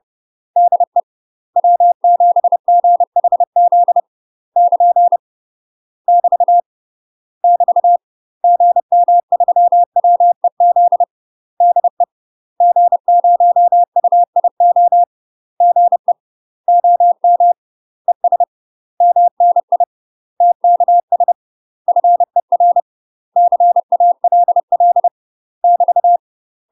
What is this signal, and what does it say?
Morse code, heard as a keyed tone, about 30 wpm, 700 Hz. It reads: DE W7GHZ <KN> <BT> <BT> GM3WEZ DE G0UIO GE OM ES MNI TKS FER CALL <BT>
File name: UK MASTER QSOs TRAINING / 30WPM / UK MASTER QSOs 0047